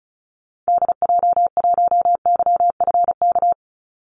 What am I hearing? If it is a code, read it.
Morse code, heard as a keyed tone, about 34 wpm, 700 Hz. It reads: DJ1YFK
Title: Call Signs - 35wpm